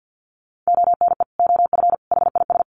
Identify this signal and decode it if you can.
Morse code, heard as a keyed tone, about 50 wpm, 700 Hz. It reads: ONE OF HIS
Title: Sets of 3 Words 50wpm